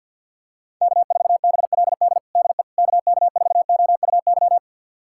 Morse code, heard as a keyed tone, about 50 wpm, 700 Hz. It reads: K4BLD DE KK4OUY
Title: USA Callsign DE Callsign 50wpm